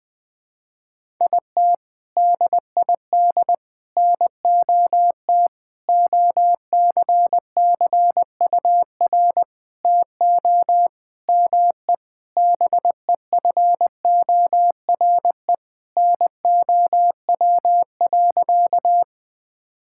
Morse code, heard as a keyed tone, about 20 wpm, 700 Hz. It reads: IT DID NOT OCCUR TO ME BEFORE NOW.
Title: Sentences from Top 1000 Words - Review Part 6 - 20wpm